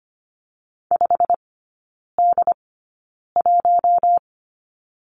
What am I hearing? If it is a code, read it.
Morse code, heard as a keyed tone, about 25 wpm, spaced as for 9 wpm, 700 Hz. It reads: 5D1
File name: course / 25:10wpm / 093 Lesson 16e - Letter c - Three Characters 25wpm